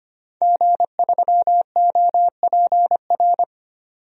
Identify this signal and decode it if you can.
Morse code, heard as a keyed tone, about 25 wpm, 700 Hz. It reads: G3OPR
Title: DXCC-100 Call Signs - 5 Characters - 25wpm